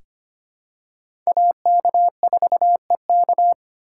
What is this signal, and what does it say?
Morse code, heard as a keyed tone, about 25 wpm, 700 Hz. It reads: AK4EK
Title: USA Call Signs - 5 Characters - 8x Character Spacing - 25wpm